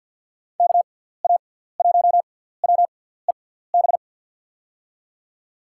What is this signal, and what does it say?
Morse code, heard as a keyed tone, about 49 wpm, spaced as for 17 wpm, 700 Hz. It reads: KA1WEB
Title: USA Call Signs - 6 Characters - 6x Character Spacing - 50wpm